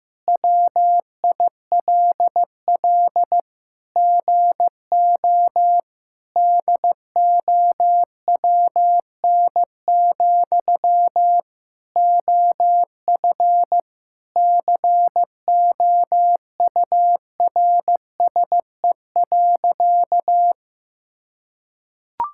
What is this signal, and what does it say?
Morse code, heard as a keyed tone, about 15 wpm, 700 Hz. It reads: WILL GO DOWN, OF COURSE.
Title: Sentences from Top 1000 Words - Review Part 5 - 15wpm